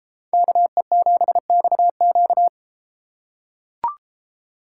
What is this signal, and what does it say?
Morse code, heard as a keyed tone, about 33 wpm, 700 Hz. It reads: KE7XQ